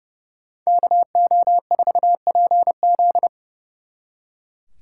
Morse code, heard as a keyed tone, about 36 wpm, 700 Hz. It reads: KO4PZ
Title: Speed Racing - USA 2x2 Call Signs 30wpm